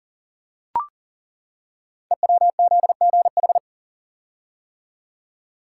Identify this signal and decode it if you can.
Morse code, heard as a keyed tone, about 38 wpm, 700 Hz. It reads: EWZGH
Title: Code Groups - 5 Letters - 40wpm